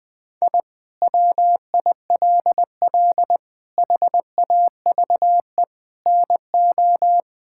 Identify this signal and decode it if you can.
Morse code, heard as a keyed tone, about 21 wpm, 700 Hz. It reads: I WILL HAVE NO
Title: Sentences from Top 1000 Words - Review Part 5 - 20wpm